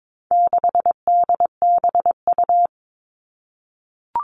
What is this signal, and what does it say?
Morse code, heard as a keyed tone, about 22 wpm, 700 Hz. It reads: 6DBU
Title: All Cut Numbers - 4-Digit 22wpm